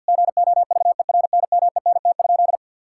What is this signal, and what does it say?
Morse code, heard as a keyed tone, about 51 wpm, 690 Hz. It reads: GOVERNMENT?
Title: Sentences from Top 1000 Words - Review Part 5 - 50wpm